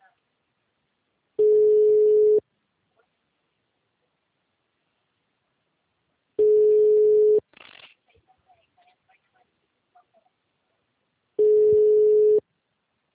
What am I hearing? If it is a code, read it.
Ringback tone (Europe, 425 Hz), with speech.
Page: Rio 3 - Ura